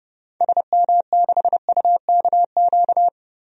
Morse code, heard as a keyed tone, about 30 wpm, 700 Hz. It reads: SM6UKQ
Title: DXCC-100 Call Signs - 6 Characters - 30wpm